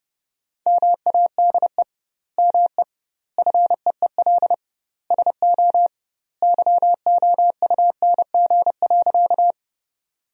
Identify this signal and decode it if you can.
Morse code, heard as a keyed tone, about 32 wpm, 700 Hz. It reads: MADE ME FEEL SO YOUNG.
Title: 2x Word-Spacing - Sentences from Top 400 Words - 30wpm